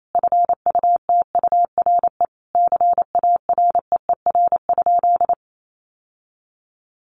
Morse code, heard as a keyed tone, about 28 wpm, 700 Hz. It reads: FUTURE CAREER?
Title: Sentences from Top 1000 Words 28wpm